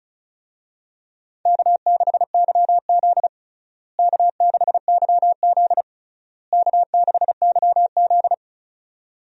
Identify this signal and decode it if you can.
Morse code, heard as a keyed tone, about 36 wpm, 700 Hz. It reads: K6YZ K6YZ K6YZ
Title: USA Call Signs - 4 Characters - Repeated 3x - 35wpm